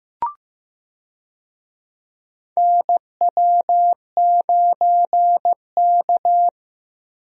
Morse code, heard as a keyed tone, about 15 wpm, 700 Hz. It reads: NW9K
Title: USA Call Signs - 4 Characters - Repeated 2x - 15wpm